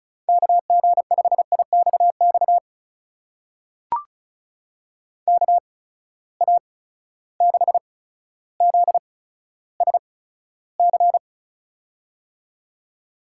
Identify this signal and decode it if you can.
Morse code, heard as a keyed tone, about 36 wpm, spaced as for 15 wpm, 700 Hz. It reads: KG5IXX KA6ZSC
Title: USA Call Signs - 6 Characters - 8x Character Spacing - 35wpm